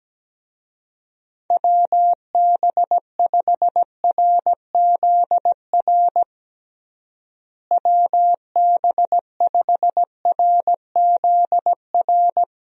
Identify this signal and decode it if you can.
Morse code, heard as a keyed tone, about 17 wpm, 700 Hz. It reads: WB5RZR WB5RZR
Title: USA Call Signs - 6 Characters - Repeated 3x - 17wpm